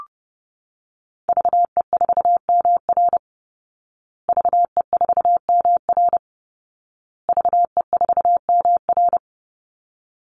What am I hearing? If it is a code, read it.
Morse code, heard as a keyed tone, about 30 wpm, 700 Hz. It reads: VE4MR VE4MR VE4MR
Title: Call Signs Repeated 2x - 30wpm